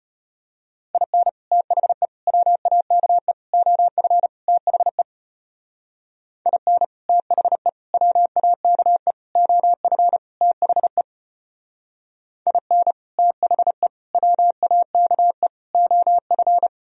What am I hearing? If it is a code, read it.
Morse code, heard as a keyed tone, about 33 wpm, 700 Hz. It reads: IN THE WAKE OF THE IN THE WAKE OF THE IN THE WAKE OF